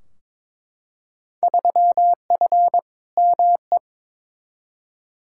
Morse code, heard as a keyed tone, about 22 wpm, 700 Hz. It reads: 3F ME
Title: Field Day Exchanges - 22wpm